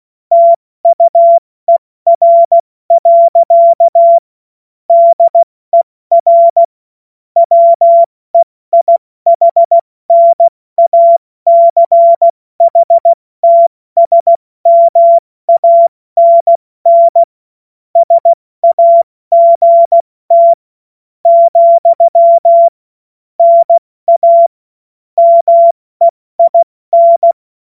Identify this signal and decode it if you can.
Morse code, heard as a keyed tone, about 16 wpm, spaced as for 14 wpm, 680 Hz. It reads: TUER. DER WEIHNACHTSMANN SAGT, NA MEIN
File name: Morsekurs / Witze / Witz 16. November 2021, 70 F80